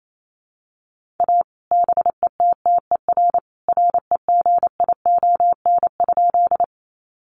Morse code, heard as a keyed tone, about 28 wpm, 700 Hz. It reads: A BETTER REGION?
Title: Sentences from Top 1500 Words 28wpm